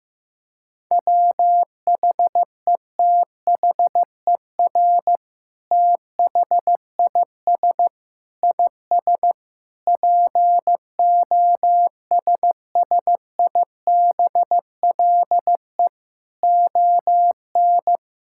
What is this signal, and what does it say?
Morse code, heard as a keyed tone, about 15 wpm, 700 Hz. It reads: WHETHER THIS IS POSSIBLE ON
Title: Sentences from Top 1000 Words - Review Part 5 - 15wpm